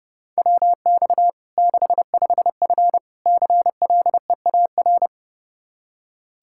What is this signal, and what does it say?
Morse code, heard as a keyed tone, about 30 wpm, 700 Hz. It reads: WX 65F CLEAR
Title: Practice QSO - Exchange 2 - Info - 30wpm